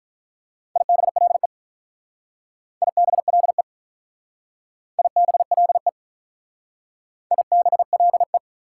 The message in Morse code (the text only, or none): IBLE IBLE IBLE IBLE